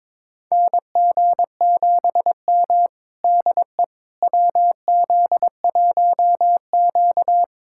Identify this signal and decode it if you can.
Morse code, heard as a keyed tone, about 22 wpm, 700 Hz. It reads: NG7M DE WZ1Q
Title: USA Callsign DE Callsign - 4 Characters - 22wpm